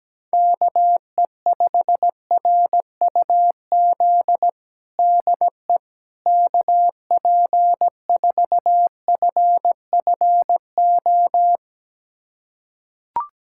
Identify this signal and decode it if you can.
Morse code, heard as a keyed tone, about 17 wpm, 700 Hz. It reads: KE5RUZ DE KP4FFO
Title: USA Callsign DE Callsign - 6 Characters - 17wpm